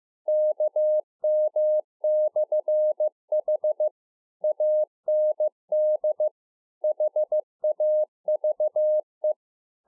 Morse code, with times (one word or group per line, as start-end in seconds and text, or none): 0.3-3.9 KM/H
4.4-6.3 AND
6.8-9.3 HAVE